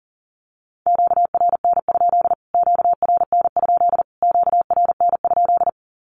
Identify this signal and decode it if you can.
Morse code, heard as a keyed tone, about 40 wpm, 700 Hz. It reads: QRN? QRN? QRN?
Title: Q-Codes Repeated 3x 40wpm